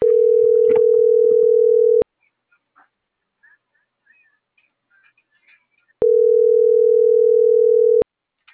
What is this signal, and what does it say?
Ringback tone (North America), 440 + 480 Hz.
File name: CENTRAL MABE / out cristopher.carrera - .3693